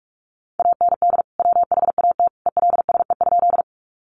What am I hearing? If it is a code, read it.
Morse code, heard as a keyed tone, about 44 wpm, 700 Hz. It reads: AND WHAT ELSE?
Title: Sentences from Top 800 Words 45wpm